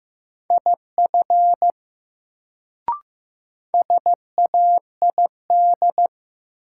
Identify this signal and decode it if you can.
Morse code, heard as a keyed tone, about 15 wpm, 700 Hz. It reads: IF SAID
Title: Sentences from Top 100 Words - New Word Drill - 15wpm